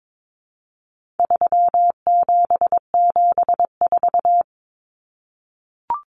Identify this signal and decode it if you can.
Morse code, heard as a keyed tone, about 22 wpm, 700 Hz. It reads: 3774
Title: Common Cut Numbers - 4-Digit 22wpm